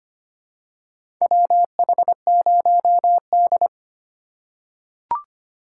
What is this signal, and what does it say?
Morse code, heard as a keyed tone, about 25 wpm, 700 Hz. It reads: WH0D
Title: USA Call Signs - 4 Characters - 3x Character Spacing - 25wpm